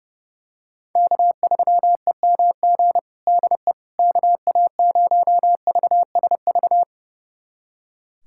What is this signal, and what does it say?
Morse code, heard as a keyed tone, about 32 wpm, 700 Hz. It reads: K3EMG DE KA0VSV
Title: USA Callsign DE Callsign 30wpm